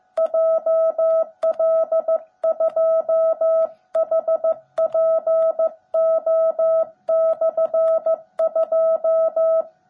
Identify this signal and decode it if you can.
Morse code, heard as a keyed tone, about 15 wpm, 650 Hz. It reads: JL2HPO/2
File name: jh8cbh / 2021cw / cw17b